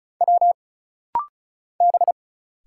Morse code, heard as a keyed tone, about 35 wpm, 700 Hz. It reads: W B
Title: Single Letters - Rapid-Fire 35wpm